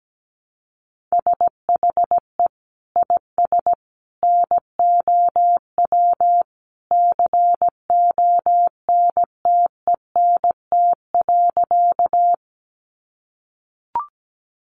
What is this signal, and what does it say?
Morse code, heard as a keyed tone, about 17 wpm, 700 Hz. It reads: SHE IS NOW CONTENT.
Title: Sentences from Top 900 Words 17wpm